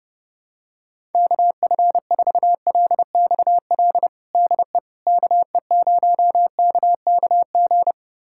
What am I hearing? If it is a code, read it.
Morse code, heard as a keyed tone, about 32 wpm, 700 Hz. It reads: KF4LXL DE KE0KKG
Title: USA Callsign DE Callsign - 6 Characters - 30wpm